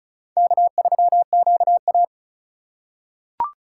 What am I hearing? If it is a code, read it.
Morse code, heard as a keyed tone, about 36 wpm, 700 Hz. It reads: K3QA
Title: USA Call Signs - 4 Characters - 4x Character Spacing - 35wpm